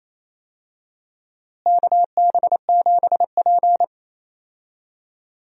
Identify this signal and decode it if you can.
Morse code, heard as a keyed tone, about 28 wpm, 700 Hz. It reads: KB7P
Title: USA Call Signs - 4 Characters - Repeated 3x - 28wpm